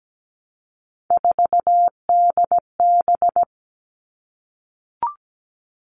Morse code, heard as a keyed tone, about 17 wpm, 700 Hz. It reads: 4DB
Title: All Cut Numbers - 3-Digit 17wpm